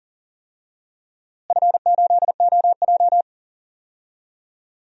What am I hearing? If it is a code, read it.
Morse code, heard as a keyed tone, about 39 wpm, 700 Hz. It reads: F8OJ
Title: DXCC-100 Call Signs - 4 Characters - Repeated 3x - 40wpm